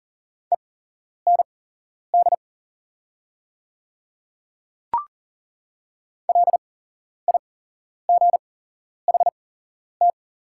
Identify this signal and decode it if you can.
Morse code, heard as a keyed tone, about 37 wpm, spaced as for 11 wpm, 700 Hz. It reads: END LIGHT
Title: 8x Character Spacing - Top 200 Words - 40wpm